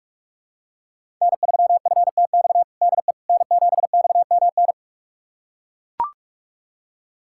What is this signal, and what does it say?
Morse code, heard as a keyed tone, about 42 wpm, 700 Hz. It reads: N3FTX DE N7XMN
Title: USA Callsign DE Callsign - 5 Characters - 45wpm